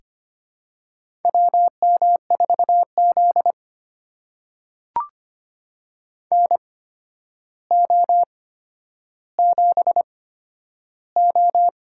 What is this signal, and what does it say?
Morse code, heard as a keyed tone, about 25 wpm, spaced as for 11 wpm, 700 Hz. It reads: WM4Z NO7O
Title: USA Call Signs - 4 Characters - 8x Character Spacing - 25wpm